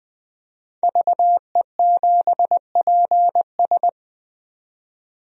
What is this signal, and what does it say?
Morse code, heard as a keyed tone, about 20 wpm, 700 Hz. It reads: VE7PS
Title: DXCC-100 Call Signs - 5 Characters - 20wpm